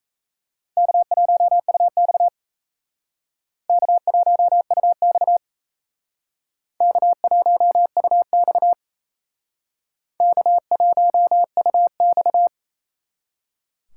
Morse code, heard as a keyed tone, about 34 wpm, 700 Hz. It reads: K1UX K1UX K1UX K1UX
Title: Speed Racing - USA 1x2 Call Signs 28wpm